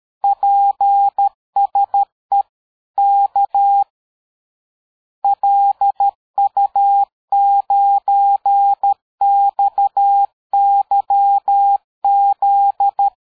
Morse code, heard as a keyed tone, about 13 wpm, 800 Hz. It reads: PSE K LU9XYZ
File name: dk5ke / QSO 60BpM